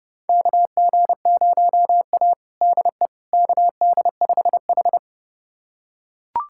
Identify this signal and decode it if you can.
Morse code, heard as a keyed tone, about 30 wpm, 700 Hz. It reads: KG0A DE KD5H
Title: USA Callsign DE Callsign - 4 Characters - 30wpm